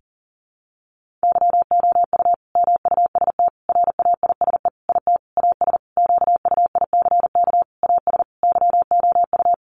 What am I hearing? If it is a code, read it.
Morse code, heard as a keyed tone, about 40 wpm, 700 Hz. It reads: YOU MUST RAISE IT AS QUICK AS YOU
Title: Sentences from Top 1400 Words 40wpm